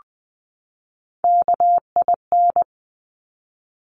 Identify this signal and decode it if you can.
Morse code, heard as a keyed tone, about 19 wpm, 700 Hz. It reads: KIN
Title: Top Three Letter Combinations in Words 20wpm